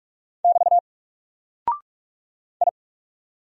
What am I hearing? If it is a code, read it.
Morse code, heard as a keyed tone, about 44 wpm, 700 Hz. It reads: <BT> I
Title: Single Character (Letter:Number:Punctuation:Prosign) Alt 45wpm